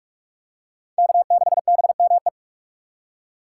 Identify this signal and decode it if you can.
Morse code, heard as a keyed tone, about 46 wpm, 700 Hz. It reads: K6BME
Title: USA Call Signs - 5 Characters - 6x Character Spacing - 45wpm